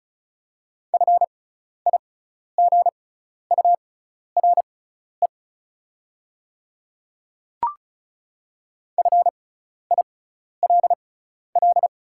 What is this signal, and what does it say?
Morse code, heard as a keyed tone, about 36 wpm, spaced as for 12 wpm, 700 Hz. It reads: FIGURE FILL
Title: 6x Character Spacing - Top 500 Words - 35wpm